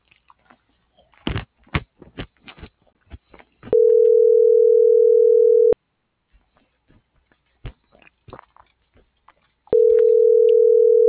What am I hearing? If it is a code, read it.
Ringback tone (North America), 440 + 480 Hz.